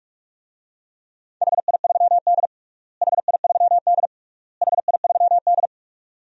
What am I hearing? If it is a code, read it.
Morse code, heard as a keyed tone, about 47 wpm, 700 Hz. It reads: HI3D HI3D HI3D
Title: DXCC-100 Call Signs - 4 Characters - Repeated 3x - 45wpm